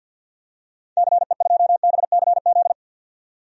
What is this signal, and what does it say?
Morse code, heard as a keyed tone, about 52 wpm, 700 Hz. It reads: CE2BCZ
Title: DXCC-100 Call Signs - 6 Characters - 50wpm